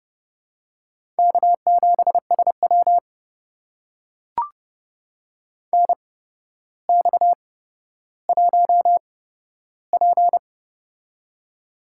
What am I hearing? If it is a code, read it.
Morse code, heard as a keyed tone, about 30 wpm, spaced as for 13 wpm, 700 Hz. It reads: K7SW NX1P
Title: USA Call Signs - 4 Characters - 8x Character Spacing - 30wpm